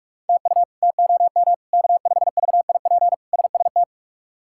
Morse code, heard as a keyed tone, about 46 wpm, 700 Hz. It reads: TU TOM K5VIP SST
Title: K1USN SST QSO Practice - 45wpm